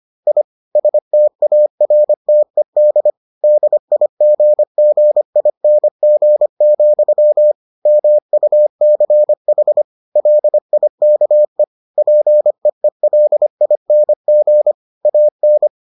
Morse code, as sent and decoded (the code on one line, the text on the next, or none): .. / ... - .- .-. - . -.. / -.. .. --. --. .. -. --. --..-- / -- ..- -.-. .... / .-.. .. -.- . / .--. . .-.. .. -. --. / .- -.
I STARTED DIGGING, MUCH LIKE PEELING AN